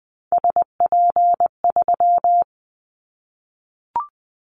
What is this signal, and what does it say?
Morse code, heard as a keyed tone, about 20 wpm, 700 Hz. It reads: SP3
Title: Callsign Prefixes for Top 25 DXCC - Repeated 3x - 20wpm